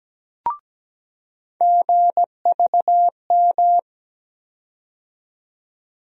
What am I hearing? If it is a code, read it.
Morse code, heard as a keyed tone, about 17 wpm, 700 Hz. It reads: GVM
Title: USA Call Sign Postfixes - 3 Letters 17wpm